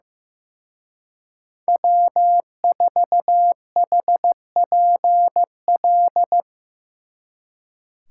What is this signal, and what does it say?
Morse code, heard as a keyed tone, about 17 wpm, 700 Hz. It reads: W4HPL
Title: Speed Racing - USA 1x3 Call Signs 15wpm